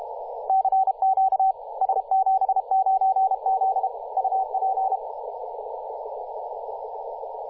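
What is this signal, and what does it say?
Morse code, heard as a keyed tone, about 32 wpm, 770 Hz, with noise. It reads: CQ S79Z UP
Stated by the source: Morse from DX station S79Z on 21.002 MHz CW, recorded by JA0RUG.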